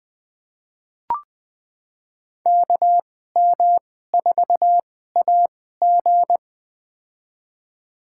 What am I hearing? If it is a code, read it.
Morse code, heard as a keyed tone, about 20 wpm, spaced as for 14 wpm, 700 Hz. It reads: KM4AG
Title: USA Call Signs - 2x Character Spacing - 20wpm